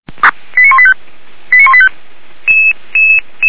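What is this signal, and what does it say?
Five-tone selective call (ZVEI), heard as quick runs of five tones, tones of about 80 ms. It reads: 89185 89185
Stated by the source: Five-tone ZVEI selective call received by SDR, from feuerwehr-orlamuende.org (Orlamünde fire brigade).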